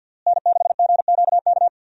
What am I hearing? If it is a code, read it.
Morse code, heard as a keyed tone, about 52 wpm, 700 Hz. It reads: N6GQK
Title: USA Call Signs - 5 Characters - Repeated 2x - 50wpm